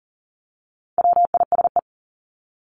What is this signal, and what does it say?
Morse code, heard as a keyed tone, about 40 wpm, 700 Hz. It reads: WISE
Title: Top 1501-1750 Words 40wpm